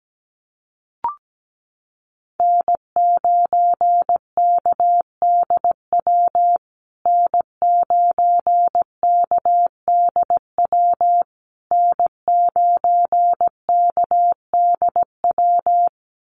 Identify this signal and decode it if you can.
Morse code, heard as a keyed tone, about 17 wpm, 700 Hz. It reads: N9KDW N9KDW N9KDW
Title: USA Call Signs Repeated 3x 17wpm